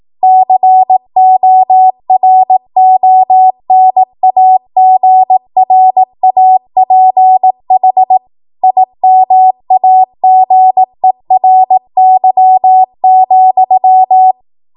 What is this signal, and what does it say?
Morse code, heard as a keyed tone, about 18 wpm, 750 Hz. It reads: CORONAGRAPH IMAGERY,
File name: Morse / 18WPM